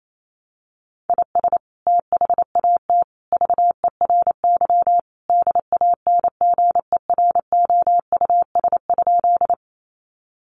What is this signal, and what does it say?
Morse code, heard as a keyed tone, about 28 wpm, 700 Hz. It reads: IS THAT VERY DANGEROUS?
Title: Sentences from Top 1200 Words 28wpm